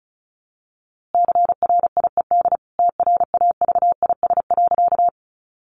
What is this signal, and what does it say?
Morse code, heard as a keyed tone, about 34 wpm, 700 Hz. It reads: CRIED TRAVIS.
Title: A Sound Of Thunder 35wpm